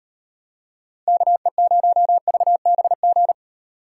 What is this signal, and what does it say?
Morse code, heard as a keyed tone, about 36 wpm, 700 Hz. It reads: KE0VBG